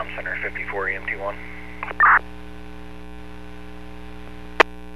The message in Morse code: none

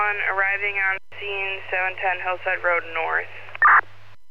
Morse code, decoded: none

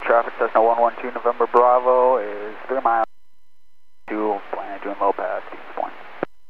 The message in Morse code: none

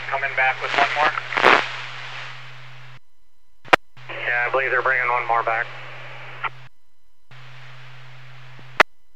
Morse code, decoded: none

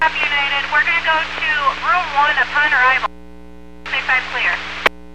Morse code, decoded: none

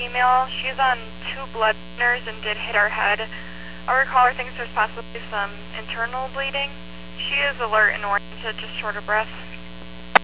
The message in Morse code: none

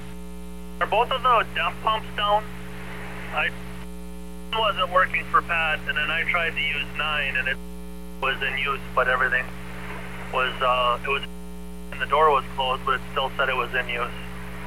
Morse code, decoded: none

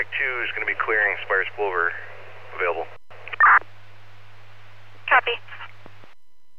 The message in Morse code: none

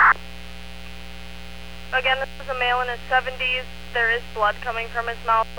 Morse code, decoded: none

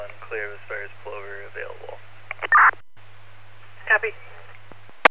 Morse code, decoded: none

